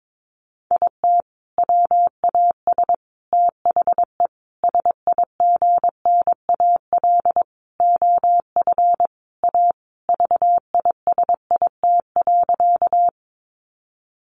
Morse code, heard as a keyed tone, about 22 wpm, 700 Hz. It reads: IT WAS THE SIGNAL OF A VISIT.
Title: Sentences from Top 1400 Words 22wpm